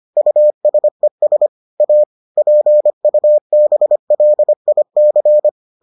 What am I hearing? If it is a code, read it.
Morse code, heard as a keyed tone, about 25 wpm, 600 Hz. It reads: USES A PUBLIC